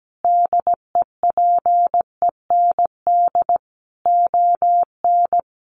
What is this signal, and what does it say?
Morse code, heard as a keyed tone, about 17 wpm, 700 Hz. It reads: DEPEND ON
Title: Sentences from Top 1300 Words 17wpm